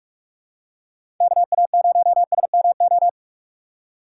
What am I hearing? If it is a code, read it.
Morse code, heard as a keyed tone, about 45 wpm, 700 Hz. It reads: KA0SMO
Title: USA Call Signs - 6 Characters - 4x Character Spacing - 45wpm